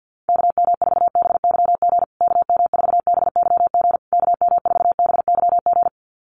Morse code, heard as a keyed tone, about 50 wpm, 700 Hz. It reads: KM4BYG KM4BYG KM4BYG
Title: USA Call Signs Repeated 3x 50wpm